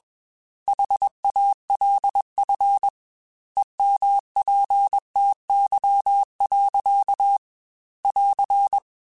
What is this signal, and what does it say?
Morse code, heard as a keyed tone, about 21 wpm, 800 Hz. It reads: HALF EMPTY. <AR>